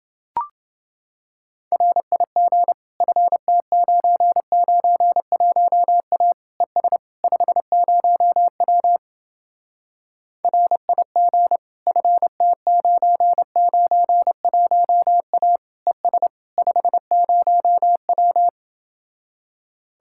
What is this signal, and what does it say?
Morse code, heard as a keyed tone, about 30 wpm, 700 Hz. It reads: RIG FT991A ES 50W RIG FT991A ES 50W